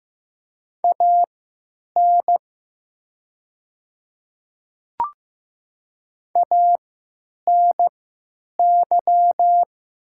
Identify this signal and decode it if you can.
Morse code, heard as a keyed tone, about 15 wpm, spaced as for 8 wpm, 700 Hz. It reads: AN ANY
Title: 3x Character Spacing - Top 200 Words - 15wpm